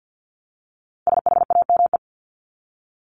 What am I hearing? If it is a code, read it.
Morse code, heard as a keyed tone, about 50 wpm, 700 Hz. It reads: SHAME
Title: Top 1751-2000 Words 50wpm